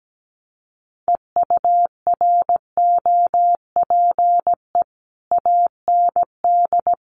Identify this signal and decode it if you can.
Morse code, heard as a keyed tone, about 17 wpm, 700 Hz. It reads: EUROPE AND
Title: Sets of 5 Words Encore2 17wpm